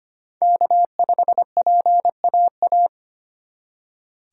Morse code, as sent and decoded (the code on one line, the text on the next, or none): -.- ..... .--. .- .-
K5PAA